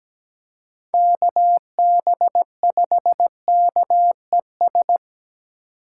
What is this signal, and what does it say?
Morse code, heard as a keyed tone, about 20 wpm, 700 Hz. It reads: KB5KES
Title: Speed Racing - USA 2x3 Call Signs 17wpm